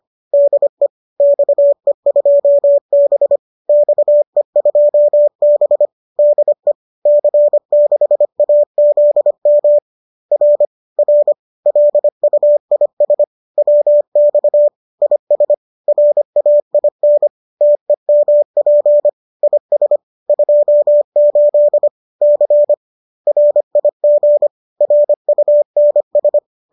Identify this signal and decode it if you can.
Morse code, heard as a keyed tone, about 25 wpm, 590 Hz. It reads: DE XE2B XE2B DE C6AZM R R LUIS WX IS RAIN TEMP IS 28 C RIG RUNS
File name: qso204 25